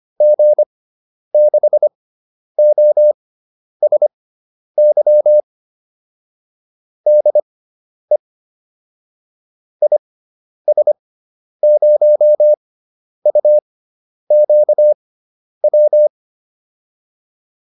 Morse code, heard as a keyed tone, about 25 wpm, spaced as for 10 wpm, 600 Hz. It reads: G6OSY DE IS0UQW
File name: QSO409 10